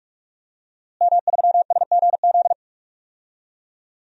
Morse code, heard as a keyed tone, about 46 wpm, 700 Hz. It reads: M3SGZ
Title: DXCC-100 Call Signs - 5 Characters - Repeated 2x - 45wpm